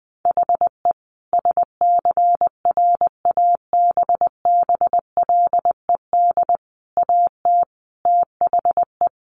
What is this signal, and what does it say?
Morse code, heard as a keyed tone, about 20 wpm, 700 Hz. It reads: HE SCRABBLED AT THE
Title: A Sound Of Thunder 20wpm